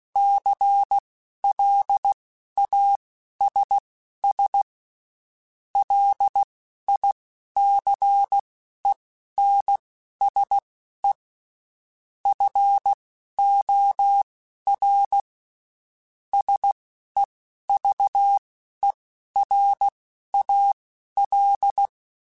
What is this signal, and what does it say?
Morse code, heard as a keyed tone, about 16 wpm, spaced as for 11 wpm, 800 Hz. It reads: CLASS LICENSE FOR SEVERAL
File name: (spaced) / QSOs Spaced-mp3s / kc7haa @15 sp